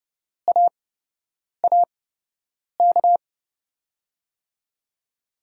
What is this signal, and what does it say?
Morse code, heard as a keyed tone, about 30 wpm, spaced as for 8 wpm, 700 Hz. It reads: AAK
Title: USA Call Signs - 5 Characters - 8x Character Spacing - 30wpm